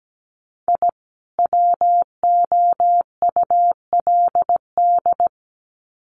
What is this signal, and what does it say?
Morse code, heard as a keyed tone, about 17 wpm, 700 Hz. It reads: I WOULD
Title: Sets of 2 Words 17wpm